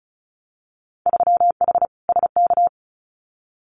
Morse code, heard as a keyed tone, about 36 wpm, 700 Hz. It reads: 3H SK
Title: Winter Field Day Exchanges 35wpm